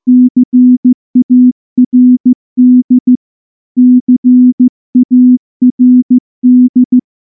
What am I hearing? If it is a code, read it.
Morse code, heard as a keyed tone, about 14 wpm, 250 Hz. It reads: CARD CARD
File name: morse / morse-training-15WPM